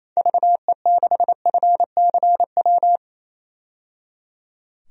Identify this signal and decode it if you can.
Morse code, heard as a keyed tone, about 27 wpm, 700 Hz. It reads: VE6FCW
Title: DXCC-100 Call Signs - 6 Characters - 28wpm